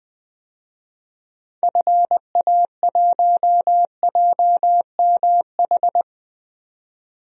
Morse code, heard as a keyed tone, about 20 wpm, 700 Hz. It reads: FA1JMH
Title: DXCC-100 Call Signs - 6 Characters - 20wpm